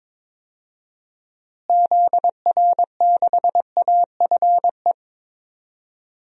Morse code, heard as a keyed tone, about 21 wpm, 700 Hz. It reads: ZR6AFE
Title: DXCC-100 Call Signs - 6 Characters - Repeated 2x - 22wpm